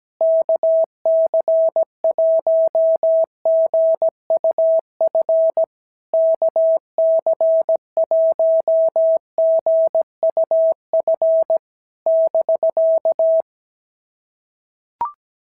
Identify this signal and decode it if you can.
Morse code, heard as a keyed tone, about 17 wpm, 660 Hz. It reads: KC1GUF KC1GUF <BK>
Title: SOTA QSO Practice 17wpm